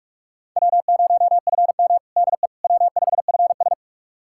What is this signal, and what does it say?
Morse code, heard as a keyed tone, about 46 wpm, 700 Hz. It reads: W0FM DE W5FS
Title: USA Callsign DE Callsign - 4 Characters - 45wpm